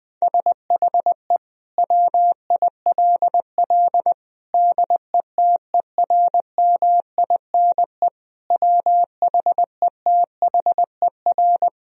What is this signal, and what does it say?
Morse code, heard as a keyed tone, about 20 wpm, 700 Hz. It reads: SHE WILL DETERMINE WHETHER
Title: Sentences from Top 1000 Words - Review Part 4 - 20wpm